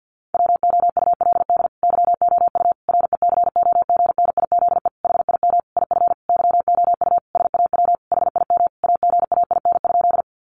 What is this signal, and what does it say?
Morse code, heard as a keyed tone, about 50 wpm, 700 Hz. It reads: WOULD YOU RECOGNIZE HIM IF YOU SAW HIM AGAIN?
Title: Sentences from Top 1400 Words 50wpm